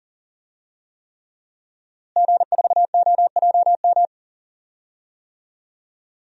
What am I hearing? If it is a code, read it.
Morse code, heard as a keyed tone, about 40 wpm, 700 Hz. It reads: G4OJM
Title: DXCC-100 Call Signs - 5 Characters - Repeated 3x - 40wpm